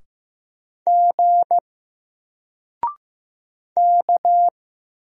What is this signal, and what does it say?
Morse code, heard as a keyed tone, about 15 wpm, 700 Hz. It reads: G K
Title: Single Letter-Number Alt 15wpm